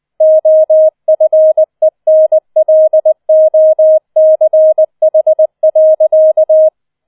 Morse code, heard as a keyed tone, about 20 wpm, 630 Hz. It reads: OFENLOCH.